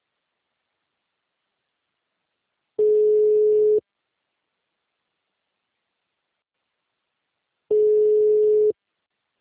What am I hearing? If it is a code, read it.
Ringback tone (Europe, 425 Hz).